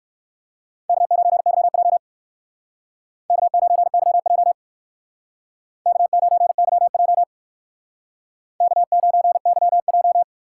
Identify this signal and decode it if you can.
Morse code, heard as a keyed tone, about 59 wpm, 700 Hz. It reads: K9YJ K9YJ K9YJ K9YJ